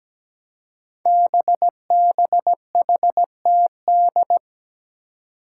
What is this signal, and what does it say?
Morse code, heard as a keyed tone, about 17 wpm, 700 Hz. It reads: BBHTD
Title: Five Challenging Letter-Number Combinations 17wpm